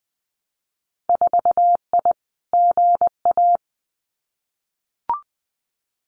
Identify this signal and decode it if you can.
Morse code, heard as a keyed tone, about 20 wpm, 700 Hz. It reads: 4I GA
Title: Winter Field Day Exchanges Encore 20wpm